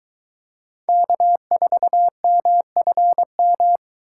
Morse code, heard as a keyed tone, about 21 wpm, 700 Hz. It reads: K4MFM